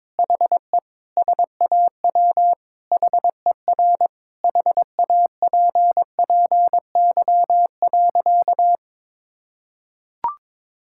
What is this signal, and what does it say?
Morse code, heard as a keyed tone, about 22 wpm, 700 Hz. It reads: HE SAW HER HAPPY.
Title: Sentences from Top 1000 Words - Review Part 6 - 22wpm